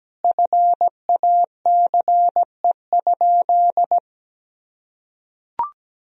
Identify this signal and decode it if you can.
Morse code, heard as a keyed tone, about 17 wpm, 700 Hz. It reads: FACE?
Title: Sentences from Top 1000 Words - Review Part 2 - 17wpm